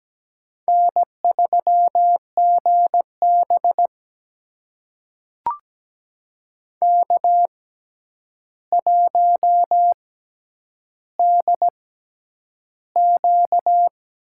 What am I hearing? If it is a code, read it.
Morse code, heard as a keyed tone, about 17 wpm, spaced as for 9 wpm, 700 Hz. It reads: N3GB K1DQ
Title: USA Call Signs - 4 Characters - 6x Character Spacing - 17wpm